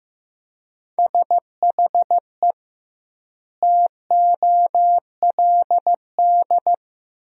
Morse code, heard as a keyed tone, about 15 wpm, 700 Hz. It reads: SHE TOLD
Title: 2x Word-Spacing - Sentences from Top 400 Words - 15wpm